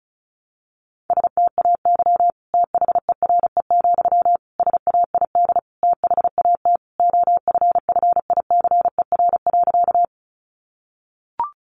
Morse code, heard as a keyed tone, about 34 wpm, 700 Hz. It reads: STAY THERE, SAID THAT OFFICER.
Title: Sentences from Top 1200 Words 35wpm